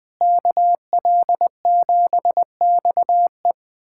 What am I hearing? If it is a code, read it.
Morse code, heard as a keyed tone, about 21 wpm, 700 Hz. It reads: KL7XE
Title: USA Callsign DE Callsign 20wpm